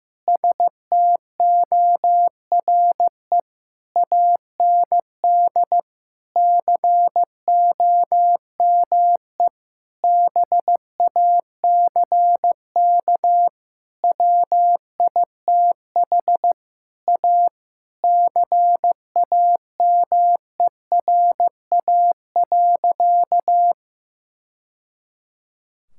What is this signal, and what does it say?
Morse code, heard as a keyed tone, about 15 wpm, 700 Hz. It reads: STORE AND COME BACK WITH A CAMERA.
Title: Sentences from Top 1000 Words - Review Part 5 - 15wpm